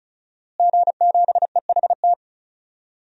Morse code, heard as a keyed tone, about 36 wpm, 700 Hz. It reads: G7EHT
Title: DXCC-100 Call Signs - 5 Characters - Repeated 2x - 35wpm